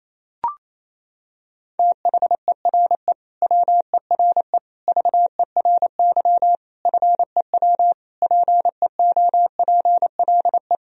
Morse code, heard as a keyed tone, about 27 wpm, 700 Hz. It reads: THERE WERE VERY FEW PEOPLE
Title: Sentences from Top 1000 Words - Review Part 2 - 28wpm